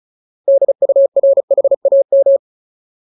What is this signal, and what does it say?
Morse code, heard as a keyed tone, about 34 wpm, 550 Hz. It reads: DURHAM
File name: Ultimate Word List / 35WPM Ultimate